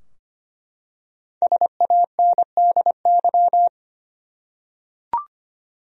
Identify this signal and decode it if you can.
Morse code, heard as a keyed tone, about 25 wpm, 700 Hz. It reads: SANDY
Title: USA Common Names - 25wpm